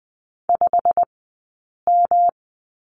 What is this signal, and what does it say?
Morse code, heard as a keyed tone, about 20 wpm, spaced as for 8 wpm, 700 Hz. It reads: 5M
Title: 211 Lesson 34d - Number 8 - Two Characters 20wpm